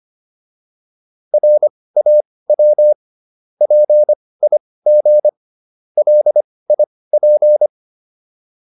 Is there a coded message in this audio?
Yes: Morse code from 1.3 to 7.7 s, heard as a keyed tone, about 25 wpm, spaced as for 18 wpm, 600 Hz. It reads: RAW PIG LIP